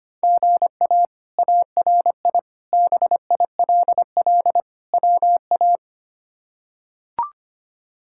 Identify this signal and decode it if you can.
Morse code, heard as a keyed tone, about 25 wpm, 700 Hz. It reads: GA ARI BILL WA
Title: K1USN SST QSO Practice - 25wpm